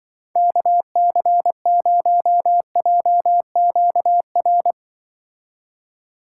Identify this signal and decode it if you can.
Morse code, heard as a keyed tone, about 24 wpm, 700 Hz. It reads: KC0JQR
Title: Speed Racing - USA 2x3 Call Signs 20wpm